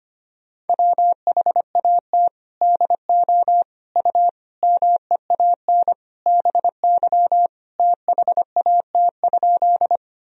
Morse code, heard as a keyed tone, about 25 wpm, 700 Hz. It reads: WHAT DO U MEAN BY THAT?
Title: Sentences from Top 200 Words and Abbreviations 25wpm